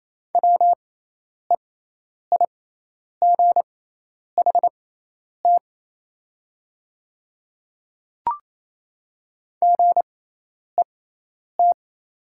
Morse code, heard as a keyed tone, about 28 wpm, spaced as for 10 wpm, 700 Hz. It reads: WEIGHT GET
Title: 6x Character Spacing - Top 500 Words - 28wpm